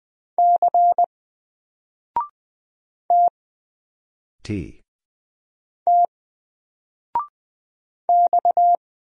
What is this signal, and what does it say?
Morse code, heard as a keyed tone, about 20 wpm, 700 Hz. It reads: C T T X
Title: Single Character (Letter:Number:Punctuation:Prosign) 20wpm